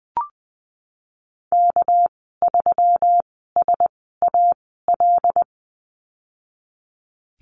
Morse code, heard as a keyed tone, about 20 wpm, spaced as for 15 wpm, 700 Hz. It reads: K3SAL
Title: 238 Lesson 38g - Letter q - Call Signs 20wpm